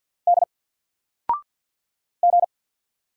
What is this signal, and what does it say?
Morse code, heard as a keyed tone, about 50 wpm, 700 Hz. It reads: D G
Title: Single Letter-Number 50wpm